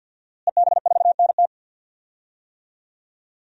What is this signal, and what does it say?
Morse code, heard as a keyed tone, about 51 wpm, 700 Hz. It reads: EB4NT
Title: DXCC-100 Call Signs - 5 Characters - Repeated 2x - 50wpm